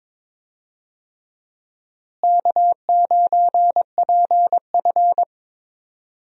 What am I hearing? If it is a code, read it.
Morse code, heard as a keyed tone, about 22 wpm, 700 Hz. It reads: K9PF
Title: USA Call Signs - 4 Characters - Repeated 3x - 22wpm